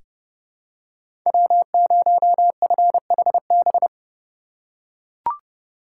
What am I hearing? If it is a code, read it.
Morse code, heard as a keyed tone, about 32 wpm, 700 Hz. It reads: W0FHB
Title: USA Call Signs - 5 Characters - 2x Character Spacing - 30wpm